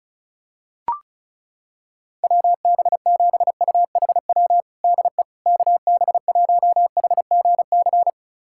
Morse code, heard as a keyed tone, about 34 wpm, 700 Hz. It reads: WB7UHW DE KB1HGC
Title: USA Callsign DE Callsign 35wpm